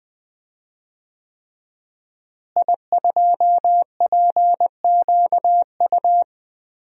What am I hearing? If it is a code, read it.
Morse code, heard as a keyed tone, about 20 wpm, 700 Hz. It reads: I2PQU
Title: DXCC-100 Call Signs - 5 Characters - Repeated 3x - 20wpm